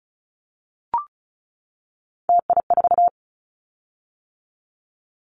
Morse code, heard as a keyed tone, about 34 wpm, 700 Hz. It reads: TI4
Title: Callsign Prefixes for Top 100 DXCC - 35wpm